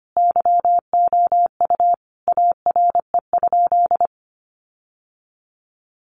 Morse code, heard as a keyed tone, about 25 wpm, 700 Hz. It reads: YOU ARE?
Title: Sentences from Top 1400 Words 25wpm